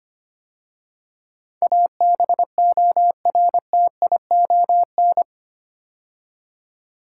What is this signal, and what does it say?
Morse code, heard as a keyed tone, about 25 wpm, 700 Hz. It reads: ABORTION
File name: 8 Letter Words - First 3 Letters Given - 25wpm